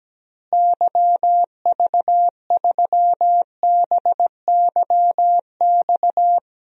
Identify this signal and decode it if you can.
Morse code, heard as a keyed tone, about 17 wpm, 700 Hz. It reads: YV3BYX
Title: DXCC-100 Call Signs - 6 Characters - Repeated 2x - 17wpm